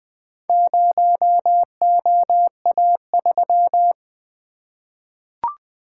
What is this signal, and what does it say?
Morse code, heard as a keyed tone, about 20 wpm, 700 Hz. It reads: 0OA3
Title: Four Letter-Number Combinations - 20wpm